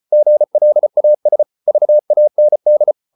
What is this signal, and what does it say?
Morse code, heard as a keyed tone, about 34 wpm, 600 Hz. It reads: GLAS VAND
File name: Ferie QRQ 34wpm